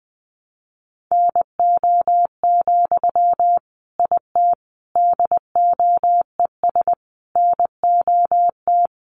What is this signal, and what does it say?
Morse code, heard as a keyed tone, about 20 wpm, 700 Hz. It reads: NO, IT DOES NOT
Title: Sentences from Top 1000 Words 20wpm